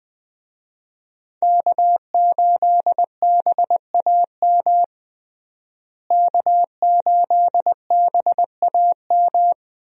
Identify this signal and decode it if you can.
Morse code, heard as a keyed tone, about 20 wpm, 700 Hz. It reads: K8BAM K8BAM
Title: USA Call Signs - 5 Characters - Repeated 2x - 20wpm